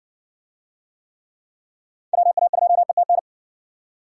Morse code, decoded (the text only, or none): WA2ETN